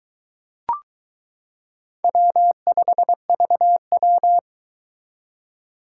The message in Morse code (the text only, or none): W5VW